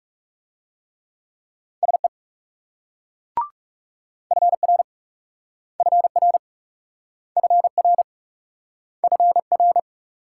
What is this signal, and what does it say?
Morse code, heard as a keyed tone, about 38 wpm, 700 Hz. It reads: SE FR FR FR FR